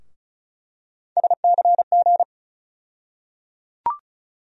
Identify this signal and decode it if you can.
Morse code, heard as a keyed tone, about 34 wpm, 700 Hz. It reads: SCG
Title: USA Call Sign Postfixes - 3 Letters 35wpm